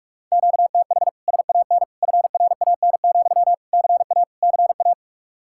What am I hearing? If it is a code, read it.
Morse code, heard as a keyed tone, about 56 wpm, 700 Hz. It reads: QTH SAN FRAN, CA CA